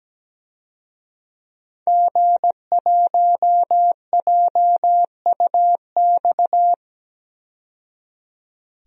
Morse code, heard as a keyed tone, about 17 wpm, 700 Hz. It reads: G1JUX
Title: DXCC-100 Call Signs - 5 Characters - 17wpm